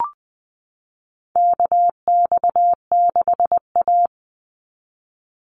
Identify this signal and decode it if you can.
Morse code, heard as a keyed tone, about 20 wpm, 690 Hz. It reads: KX6A
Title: NA SOTA Chaser Callsigns 20wpm